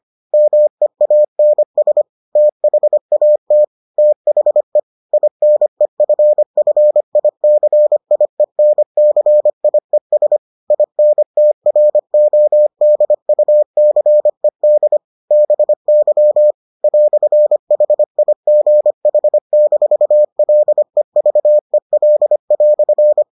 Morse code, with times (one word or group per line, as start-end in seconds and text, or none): 0.3-2.0 MEANS
2.4-3.6 THAT
4.0-4.8 THE
5.1-10.4 INEFFICIENCIES
10.7-15.0 INTRODUCED
15.3-16.5 BY
16.8-23.2 "HIGH-LEVEL"